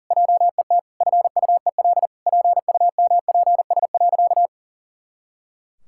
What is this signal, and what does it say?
Morse code, heard as a keyed tone, about 40 wpm, 700 Hz. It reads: JET FUEL PUMPS.
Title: Pangram Challenge v2 - 40wpm